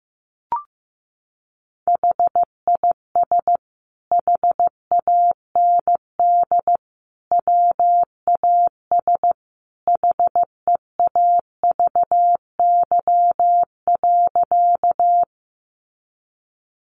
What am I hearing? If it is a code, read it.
Morse code, heard as a keyed tone, about 15 wpm, 700 Hz. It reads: HIS HAND WAS HEAVY.
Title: Sentences from Top 1100 Words 15wpm